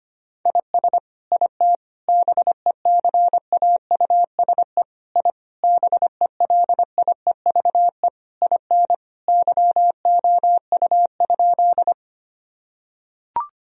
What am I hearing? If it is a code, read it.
Morse code, heard as a keyed tone, about 25 wpm, 700 Hz. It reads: IS IT BECAUSE I BELIEVE IN YOU?
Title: Sentences from Top 1000 Words - Review Part 3 - 25wpm